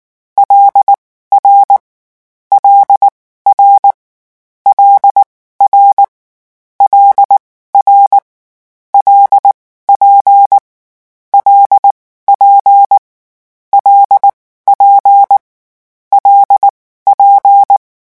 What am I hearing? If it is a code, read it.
Morse code, heard as a keyed tone, about 19 wpm, spaced as for 14 wpm, 800 Hz. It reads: LR LR L R LR LP L P LP L P
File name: morse mp3 / lecon6-vitesse 18